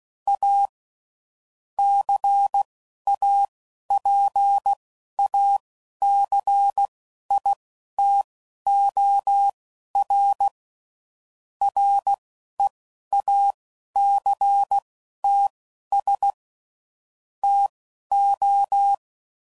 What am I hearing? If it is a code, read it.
Morse code, heard as a keyed tone, about 16 wpm, spaced as for 11 wpm, 800 Hz. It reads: A CAPACITOR REACTS TO
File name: CW Code Groups @10WPM / Technical Subjects / XC